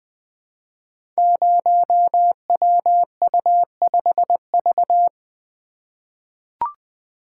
Morse code, heard as a keyed tone, about 20 wpm, 700 Hz. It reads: 0WU5V
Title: Code Groups - 5 Letter:Numbers - 20wpm